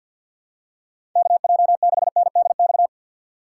Morse code, heard as a keyed tone, about 53 wpm, 700 Hz. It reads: KJ6NDX
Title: USA Call Signs - 6x Character Spacing - 50wpm